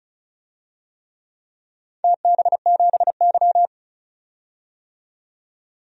Morse code, heard as a keyed tone, about 36 wpm, 700 Hz. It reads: TB7Y
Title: DXCC-100 Call Signs - 4 Characters - 35wpm